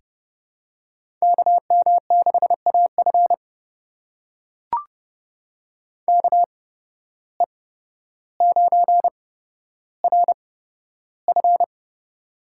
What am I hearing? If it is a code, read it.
Morse code, heard as a keyed tone, about 32 wpm, spaced as for 13 wpm, 700 Hz. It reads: KM6AF KE9RF
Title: USA Call Signs - 5 Characters - 8x Character Spacing - 30wpm